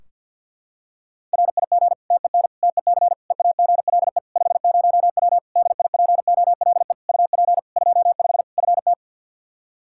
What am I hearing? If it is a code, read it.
Morse code, heard as a keyed tone, about 50 wpm, 700 Hz. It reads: RIG TEN TEC EAGLE 50W DIPOLE UP 25 FT
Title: Practice QSO - Exchange 2 - Info - 50wpm